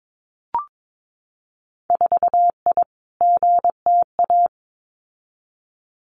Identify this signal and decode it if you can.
Morse code, heard as a keyed tone, about 22 wpm, 700 Hz. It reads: 4I GTA
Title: Winter Field Day Exchanges Encore 22wpm